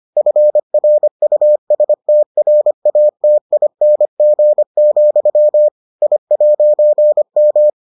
Morse code, heard as a keyed tone, about 25 wpm, 600 Hz. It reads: FRUSTRATING, I'M